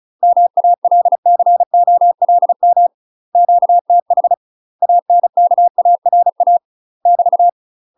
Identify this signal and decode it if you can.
Morse code, heard as a keyed tone, about 36 wpm, 700 Hz. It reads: MALCOLM QTH ANKARA <BT>
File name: UK MASTER QSOs TRAINING / 35WPM / UK MASTER QSOs 0029